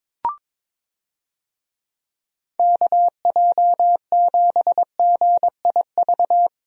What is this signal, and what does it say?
Morse code, heard as a keyed tone, about 22 wpm, 700 Hz. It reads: KJ7GIV
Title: USA Call Signs - 6 Characters - Repeated 2x - 22wpm